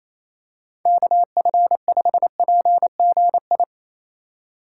Morse code, heard as a keyed tone, about 27 wpm, 700 Hz. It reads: KF5PGI
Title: USA Call Signs - 6 Characters - 4x Character Spacing - 28wpm